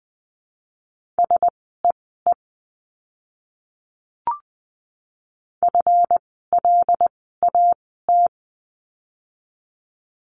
Morse code, heard as a keyed tone, about 20 wpm, spaced as for 15 wpm, 700 Hz. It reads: SEE FLAT
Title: 139 Lesson 23f - Letter f - Words 20wpm